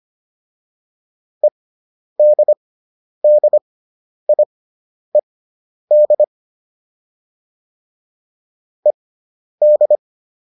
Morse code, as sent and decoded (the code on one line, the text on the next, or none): . -.. -.. .. -.. / . -..
EDDIED ED